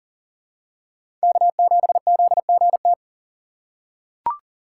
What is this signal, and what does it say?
Morse code, heard as a keyed tone, about 40 wpm, 700 Hz. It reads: K7ZGT
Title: USA Call Signs - 5 Characters - 2x Character Spacing - 40wpm